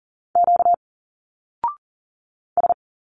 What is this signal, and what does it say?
Morse code, heard as a keyed tone, about 39 wpm, 700 Hz. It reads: Q S
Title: Single Letters 40wpm